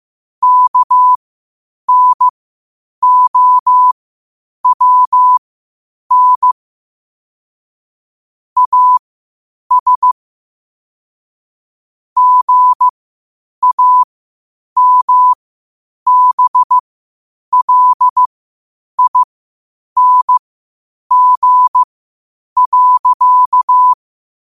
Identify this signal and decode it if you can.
Morse code, heard as a keyed tone, about 15 wpm, spaced as for 8 wpm, 1000 Hz. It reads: KNOWN AS GAMBLING.
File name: qotdfiles / 8wpm